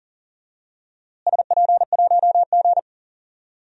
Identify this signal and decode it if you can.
Morse code, heard as a keyed tone, about 40 wpm, 700 Hz. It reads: SP1G
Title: DXCC-100 Call Signs - 4 Characters - Repeated 2x - 40wpm